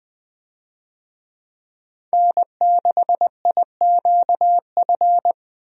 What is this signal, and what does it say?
Morse code, heard as a keyed tone, about 20 wpm, 700 Hz. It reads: N6IQF
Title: USA Call Signs - 5 Characters - Repeated 3x - 20wpm